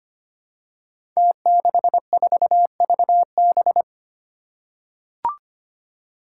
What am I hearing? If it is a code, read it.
Morse code, heard as a keyed tone, about 25 wpm, 700 Hz. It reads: T64VB
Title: Five Challenging Letter-Number Combinations 25wpm